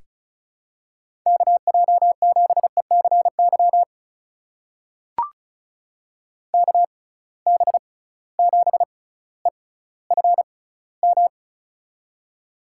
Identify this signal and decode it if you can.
Morse code, heard as a keyed tone, about 36 wpm, spaced as for 18 wpm, 690 Hz. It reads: KJ7ECY KB7EFM
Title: USA Call Signs - 6 Characters - 6x Character Spacing - 35wpm